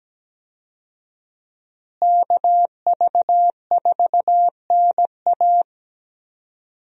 Morse code, heard as a keyed tone, about 17 wpm, 700 Hz. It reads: KV4NA
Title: USA Call Signs - 5 Characters - Repeated 2x - 17wpm